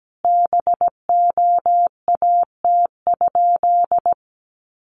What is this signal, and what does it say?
Morse code, heard as a keyed tone, about 17 wpm, 700 Hz. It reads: BOAT?
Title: Sentences from Top 1300 Words 17wpm